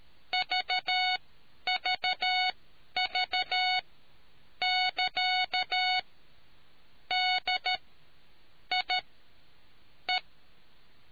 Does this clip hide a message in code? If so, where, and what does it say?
Yes: Morse code from 0.3 to 10.2 s, heard as a keyed tone, about 13 wpm, spaced as for 7 wpm, 740 Hz. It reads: VVV<KA>DIE